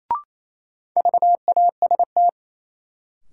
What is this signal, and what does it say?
Morse code, heard as a keyed tone, about 28 wpm, 700 Hz. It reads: VAST
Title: Sets of 4 Words - New Word Drill - 28wpm